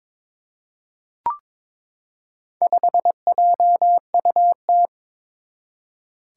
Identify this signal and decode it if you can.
Morse code, heard as a keyed tone, about 22 wpm, 700 Hz. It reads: 5JUT
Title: Four Challenging Letter-Number Combinations 22wpm